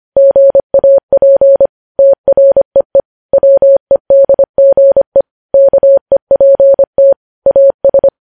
Morse code, heard as a keyed tone, about 25 wpm, 550 Hz. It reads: GAP TREE WEDGE KEPT AS